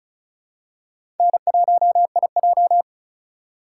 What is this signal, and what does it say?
Morse code, heard as a keyed tone, about 36 wpm, 700 Hz. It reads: N1IJ
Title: USA Call Signs - 4 Characters - 4x Character Spacing - 35wpm